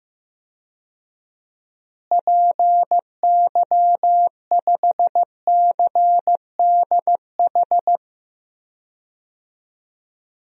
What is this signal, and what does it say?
Morse code, heard as a keyed tone, about 15 wpm, 700 Hz. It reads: PY5CDH
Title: DXCC-100 Call Signs - 6 Characters - Repeated 3x - 15wpm